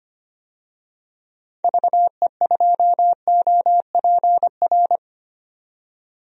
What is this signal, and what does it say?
Morse code, heard as a keyed tone, about 25 wpm, 700 Hz. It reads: VE2OPR
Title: DXCC-100 Call Signs - 6 Characters - 25wpm